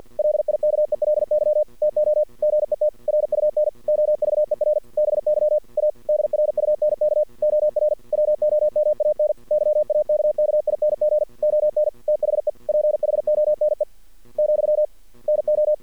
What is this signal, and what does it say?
Morse code, heard as a keyed tone, about 50 wpm, 600 Hz. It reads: LIKELY TO GET RUN OVER BY A DRUNK OR WOMAN YAKKING ON THE PHONE , NO
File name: archived / CRSnet